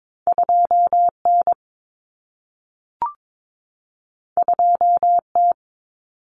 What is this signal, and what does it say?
Morse code, heard as a keyed tone, about 22 wpm, 700 Hz. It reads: 2N 2T
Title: Common Cut Numbers - 2-Digit 22wpm